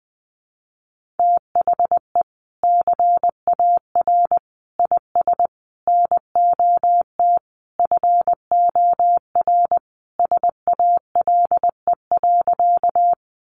Morse code, heard as a keyed tone, about 20 wpm, 700 Hz. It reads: THE CAR IS NOT FOR SALE.
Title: Sentences from Top 1400 Words 20wpm